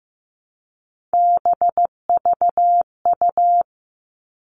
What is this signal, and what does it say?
Morse code, heard as a keyed tone, about 15 wpm, 700 Hz. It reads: BVU
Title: All Cut Numbers - 3-Digit 15wpm